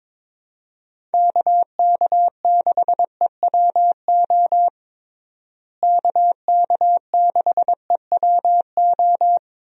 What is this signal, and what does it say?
Morse code, heard as a keyed tone, about 22 wpm, 700 Hz. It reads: KK6EWO KK6EWO
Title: USA Call Signs - 6 Characters - Repeated 2x - 22wpm